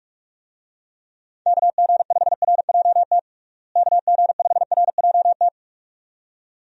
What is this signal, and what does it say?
Morse code, heard as a keyed tone, about 46 wpm, 700 Hz. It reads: KG5RJT KG5RJT
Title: USA Call Signs - 6 Characters - Repeated 2x - 45wpm